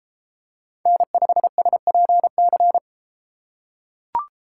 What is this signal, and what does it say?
Morse code, heard as a keyed tone, about 31 wpm, 700 Hz. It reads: N5SPC